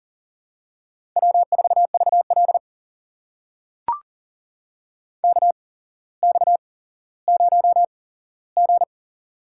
Morse code, heard as a keyed tone, about 38 wpm, spaced as for 17 wpm, 700 Hz. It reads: W4VL KX0G
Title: USA Call Signs - 4 Characters - 8x Character Spacing - 40wpm